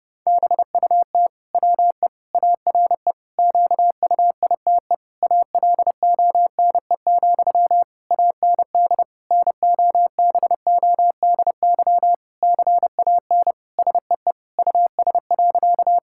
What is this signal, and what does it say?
Morse code, heard as a keyed tone, about 30 wpm, 700 Hz. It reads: BUT WE ARE QUITE ALONE, AND NOBODY CAN SEE US.
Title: Sentences from Top 1600 Words 30wpm